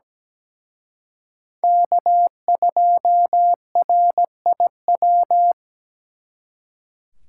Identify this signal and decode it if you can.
Morse code, heard as a keyed tone, about 20 wpm, 700 Hz. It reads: K2RIW
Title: Speed Racing - USA Call Signs 17wpm